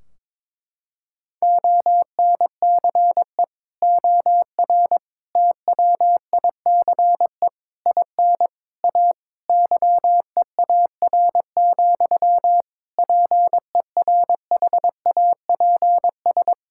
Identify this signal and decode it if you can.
Morse code, heard as a keyed tone, about 22 wpm, 700 Hz. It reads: ONCE OR TWICE IN A YEAR, PERHAPS
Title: Sentences from Top 1600 Words 22wpm